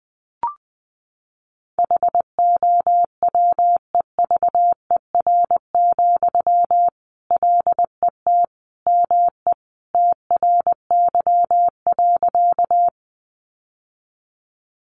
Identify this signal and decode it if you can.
Morse code, heard as a keyed tone, about 20 wpm, 700 Hz. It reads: HOWEVER, LET ME TRY.
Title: Sentences from Top 600 Words 20wpm